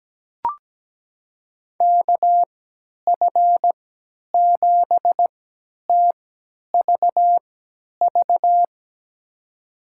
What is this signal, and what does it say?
Morse code, heard as a keyed tone, about 17 wpm, spaced as for 10 wpm, 700 Hz. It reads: KF7TVV
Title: USA Call Signs - 6 Characters - 3x Character Spacing - 17wpm